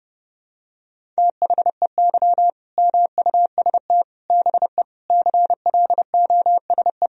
Morse code, heard as a keyed tone, about 30 wpm, 700 Hz. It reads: THEY MUST BE CLOSE
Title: Sentences from Top 1000 Words - Review Part 1 - 30wpm